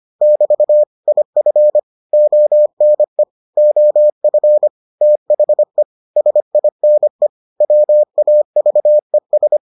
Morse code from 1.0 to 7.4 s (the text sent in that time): IF ONE OF THE SINE